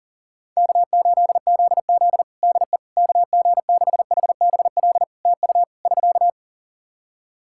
Morse code, heard as a keyed tone, about 40 wpm, 700 Hz. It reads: K8ZZ DE KG6HBL TU <SK>
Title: Practice QSO - Exchange 4 - Ending - 40wpm